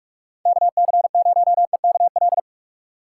Morse code, heard as a keyed tone, about 45 wpm, 700 Hz. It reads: KC0EKL
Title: USA Call Signs - 6 Characters - 3x Character Spacing - 45wpm